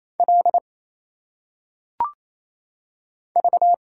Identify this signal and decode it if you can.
Morse code, heard as a keyed tone, about 27 wpm, 700 Hz. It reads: L V
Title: Single Letters - Phonetic Pronunciation 28wpm